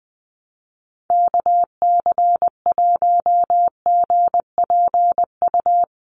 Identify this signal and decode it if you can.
Morse code, heard as a keyed tone, about 20 wpm, 700 Hz. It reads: KC1GPU
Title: USA Call Signs Repeated 2x 20wpm